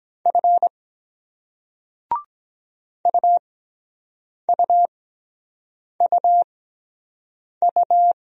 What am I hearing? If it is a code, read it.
Morse code, heard as a keyed tone, about 23 wpm, spaced as for 7 wpm, 700 Hz. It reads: F UUUU